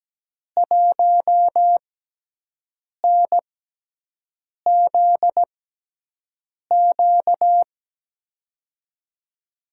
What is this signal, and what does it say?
Morse code, heard as a keyed tone, about 17 wpm, spaced as for 6 wpm, 700 Hz. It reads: 1NZQ
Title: USA Call Signs - 5 Characters - 6x Character Spacing - 17wpm